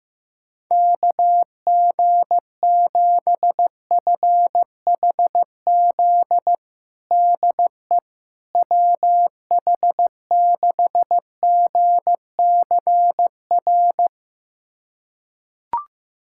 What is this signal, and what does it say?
Morse code, heard as a keyed tone, about 16 wpm, 700 Hz. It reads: KG7FHZ DE WH6GCR
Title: USA Callsign DE Callsign - 6 Characters - 15wpm